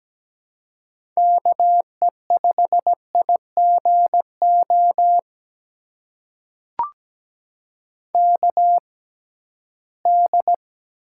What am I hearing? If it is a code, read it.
Morse code, heard as a keyed tone, about 17 wpm, spaced as for 13 wpm, 700 Hz. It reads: KE5IGO KD
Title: USA Call Signs - 6 Characters - 6x Character Spacing - 17wpm